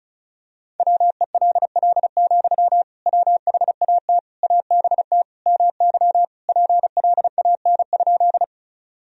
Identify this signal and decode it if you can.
Morse code, heard as a keyed tone, about 34 wpm, 700 Hz. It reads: WELL, WHAT ABT MY PLAN?
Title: Sentences from Top 500 Words and Abbreviations 35wpm